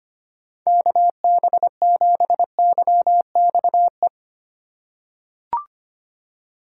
Morse code, heard as a keyed tone, about 25 wpm, 700 Hz. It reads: KB7YXE
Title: USA Call Signs - 6 Characters - 3x Character Spacing - 25wpm